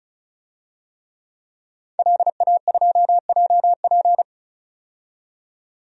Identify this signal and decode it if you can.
Morse code, heard as a keyed tone, about 34 wpm, 700 Hz. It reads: LA2JP
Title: DXCC-100 Call Signs - 5 Characters - 35wpm